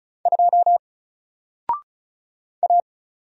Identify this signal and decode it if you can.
Morse code, heard as a keyed tone, about 34 wpm, 700 Hz. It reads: 2 A
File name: Single-Letter-Number-35wpm